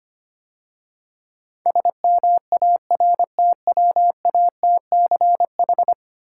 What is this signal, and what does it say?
Morse code, heard as a keyed tone, about 26 wpm, 700 Hz. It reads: SMARTWATCH
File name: Easy Words - Speed-Racing with Hints - 25wpm